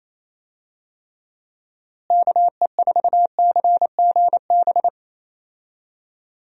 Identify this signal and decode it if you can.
Morse code, heard as a keyed tone, about 28 wpm, 700 Hz. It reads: KE4CGB
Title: USA Call Signs - 6 Characters - Repeated 3x - 28wpm